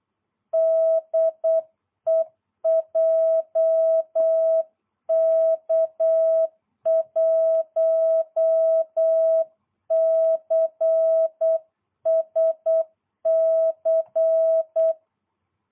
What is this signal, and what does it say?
Morse code, heard as a keyed tone, about 8 wpm, 660 Hz. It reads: DE JK1CSC